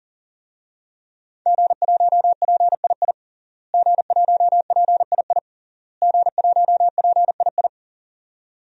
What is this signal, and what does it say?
Morse code, heard as a keyed tone, about 40 wpm, 700 Hz. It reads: G1PII G1PII G1PII
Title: DXCC-100 Call Signs - 5 Characters - Repeated 3x - 40wpm